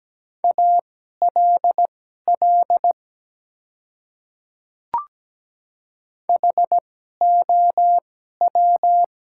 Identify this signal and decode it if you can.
Morse code, heard as a keyed tone, about 17 wpm, spaced as for 12 wpm, 700 Hz. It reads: ALL HOW
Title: 2x Character Spacing - Top 100 Words - 17wpm